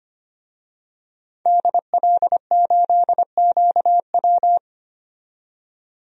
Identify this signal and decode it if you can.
Morse code, heard as a keyed tone, about 25 wpm, 700 Hz. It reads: DL8QW
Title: DXCC-100 Call Signs - 5 Characters - 25wpm